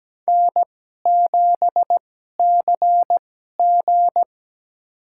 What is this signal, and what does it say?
Morse code, heard as a keyed tone, about 17 wpm, spaced as for 12 wpm, 700 Hz. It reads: N7CG
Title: USA Call Signs - 4 Characters - 2x Character Spacing - 17wpm